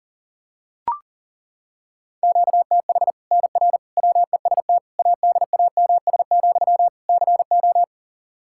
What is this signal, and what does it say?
Morse code, heard as a keyed tone, about 40 wpm, 700 Hz. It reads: QTH NR WEST ADAMS, CO
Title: Practice QSO - Exchange 1 - Beginning - 40wpm